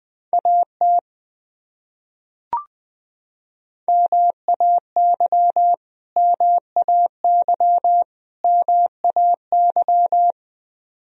Morse code, heard as a keyed tone, about 20 wpm, 700 Hz. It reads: AT MAY MAY MAY
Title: Top 100 Words Repeated 20wpm